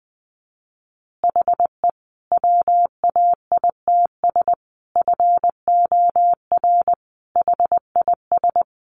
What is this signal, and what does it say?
Morse code, heard as a keyed tone, about 20 wpm, 700 Hz. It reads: HE WAITS FOR HIS
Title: Additional Easy Phrases 20wpm